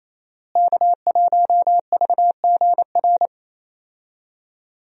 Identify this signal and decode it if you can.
Morse code, heard as a keyed tone, about 28 wpm, 700 Hz. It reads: K1VGR
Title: USA Call Signs - 5 Characters - Repeated 2x - 28wpm